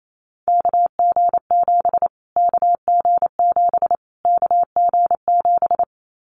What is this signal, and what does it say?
Morse code, heard as a keyed tone, about 28 wpm, 700 Hz. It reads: KG7 KG7 KG7
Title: Callsign Prefixes for Top 25 DXCC - Repeated 3x - 28wpm